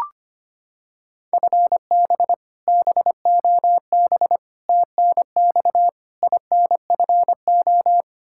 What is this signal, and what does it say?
Morse code, heard as a keyed tone, about 25 wpm, 700 Hz. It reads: FB BOB TNX INFO
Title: Practice QSO - Exchange 3 - More Info - 25wpm